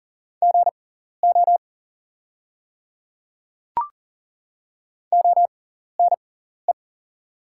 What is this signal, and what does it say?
Morse code, heard as a keyed tone, about 40 wpm, spaced as for 14 wpm, 690 Hz. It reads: GO ONE
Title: 6x Character Spacing - Top 100 Words - 40wpm